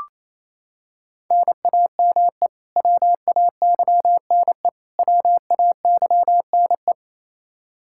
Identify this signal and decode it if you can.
Morse code, heard as a keyed tone, about 28 wpm, 700 Hz. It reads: NAME WAYNE WAYNE
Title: Practice QSO - Exchange 1 - Beginning - 28wpm